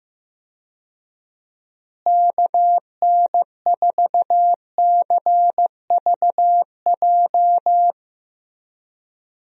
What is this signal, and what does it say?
Morse code, heard as a keyed tone, about 15 wpm, 700 Hz. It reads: KN4CVJ
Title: USA Call Signs - 6 Characters - Repeated 2x - 15wpm